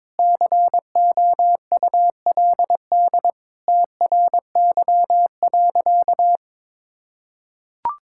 Morse code, heard as a keyed tone, about 22 wpm, 700 Hz. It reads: COULD TRY.
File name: practice / 22wpm / Sentences with 2-Word Phrases - 22wpm